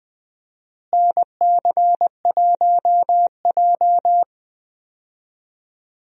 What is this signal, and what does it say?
Morse code, heard as a keyed tone, about 20 wpm, 700 Hz. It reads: NC1J
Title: USA Call Signs - 4 Characters - Repeated 3x - 20wpm